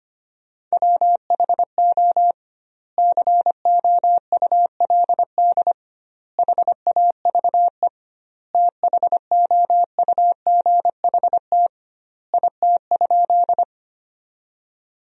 Morse code, heard as a keyed tone, about 25 wpm, 700 Hz. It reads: WHO COULD HAVE THOUGHT IT?
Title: 2x Word-Spacing - Sentences from Top 300 Words - 25wpm